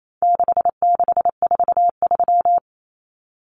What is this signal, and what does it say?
Morse code, heard as a keyed tone, about 28 wpm, 700 Hz. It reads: 6643
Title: Common Cut Numbers - 4-Digit 28wpm